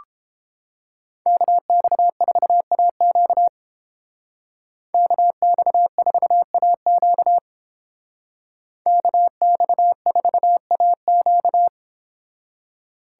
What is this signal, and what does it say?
Morse code, heard as a keyed tone, about 31 wpm, 700 Hz. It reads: KX4AQ KX4AQ KX4AQ